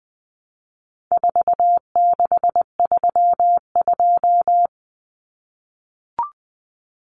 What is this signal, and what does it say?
Morse code, heard as a keyed tone, about 20 wpm, 700 Hz. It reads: 4632
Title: Common Cut Numbers - 4-Digit 20wpm